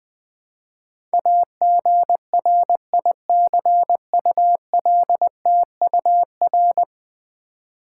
Morse code, heard as a keyed tone, about 20 wpm, 700 Hz. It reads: AGRICULTUR
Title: Build A Word - 5 Letters At A Time - 12-Letter Words - 20wpm